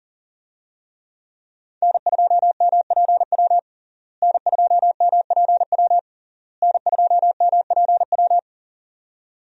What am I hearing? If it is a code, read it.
Morse code, heard as a keyed tone, about 40 wpm, 700 Hz. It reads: N2MPW N2MPW N2MPW
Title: USA Call Signs - 5 Characters - Repeated 3x - 40wpm